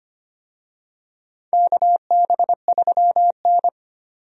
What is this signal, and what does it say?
Morse code, heard as a keyed tone, about 25 wpm, 700 Hz. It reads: KB3N
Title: USA Call Signs - 4 Characters - 25wpm